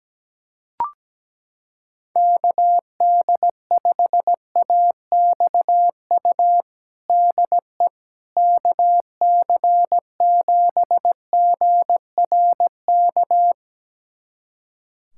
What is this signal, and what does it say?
Morse code, heard as a keyed tone, about 17 wpm, 700 Hz. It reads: KD5AXU DE KC7GRK
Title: USA Callsign DE Callsign - 6 Characters - 17wpm